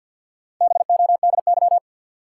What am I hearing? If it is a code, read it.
Morse code, heard as a keyed tone, about 53 wpm, 700 Hz. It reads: BODY
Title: Sentences from Top 400 Words - New Word Drill - 50wpm